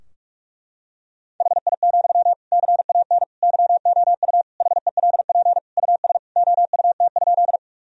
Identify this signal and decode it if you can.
Morse code, heard as a keyed tone, about 46 wpm, 700 Hz. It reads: HI, CAN YOU HELP US OUT?
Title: Sentences from Top 1000 Words - Review Part 3 - 45wpm